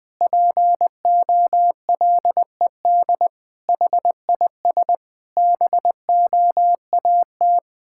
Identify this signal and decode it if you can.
Morse code, heard as a keyed tone, about 20 wpm, 700 Hz. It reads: POLED HIS BOAT
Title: Pangram Challenge v2 - 20wpm